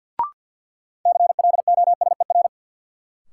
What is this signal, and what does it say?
Morse code, heard as a keyed tone, about 48 wpm, 700 Hz. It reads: CLOSER
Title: Sets of 5 Words Encore - New Word Drill - 50wpm